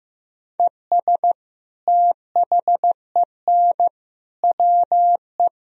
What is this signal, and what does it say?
Morse code, heard as a keyed tone, about 15 wpm, 700 Hz. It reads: ES THEN WE
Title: Sentences from Top 300 Words and Abbreviations 15wpm